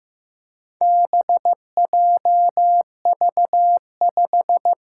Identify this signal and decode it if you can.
Morse code, heard as a keyed tone, about 15 wpm, 700 Hz. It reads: BJV5
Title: Four Challenging Letter-Number Combinations 15wpm